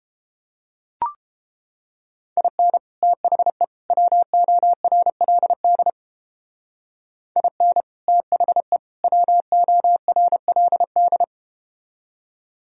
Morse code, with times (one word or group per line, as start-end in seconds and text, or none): 2.4-2.8 IN
3.0-3.6 THE
3.9-5.9 WORLD
7.4-7.8 IN
8.1-8.8 THE
9.0-11.2 WORLD